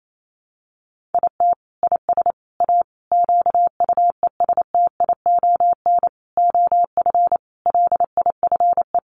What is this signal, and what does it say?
Morse code, heard as a keyed tone, about 28 wpm, 700 Hz. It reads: IT IS A QUESTION OF LIFE
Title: Sentences from Top 1100 Words 28wpm